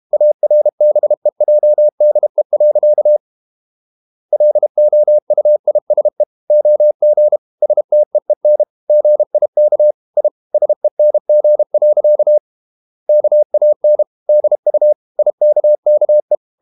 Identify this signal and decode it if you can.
Morse code, heard as a keyed tone, about 32 wpm, 600 Hz. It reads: ARBEJDE. LOUISE OG STEEN GIK I SENG. KAN DU IKKE
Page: Ferie 32 wpm